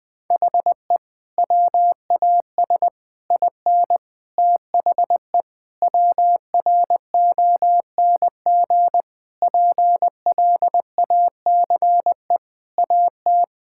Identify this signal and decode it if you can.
Morse code, heard as a keyed tone, about 20 wpm, 700 Hz. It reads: HE WAS IN THE WRONG PLACE AT
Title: Sentences with 8-Word Phrases - 20wpm